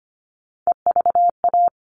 Morse code, heard as a keyed tone, about 25 wpm, 700 Hz. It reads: EVA
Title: All Cut Numbers - 3-Digit 25wpm